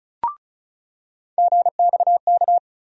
Morse code, heard as a keyed tone, about 36 wpm, 700 Hz. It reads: GXK
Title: USA Call Sign Postfixes - 3 Letters 35wpm